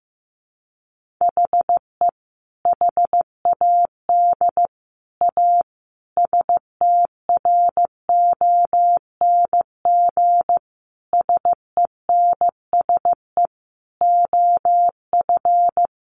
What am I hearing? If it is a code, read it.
Morse code, heard as a keyed tone, about 15 wpm, 700 Hz. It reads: HE HAD A STRONG SENSE OF
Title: Sentences from Top 1500 Words 15wpm